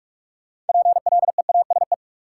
Morse code, heard as a keyed tone, about 44 wpm, 700 Hz. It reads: PLEASE
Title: Sentences from Top 1500 Words - New Word Drill - 45wpm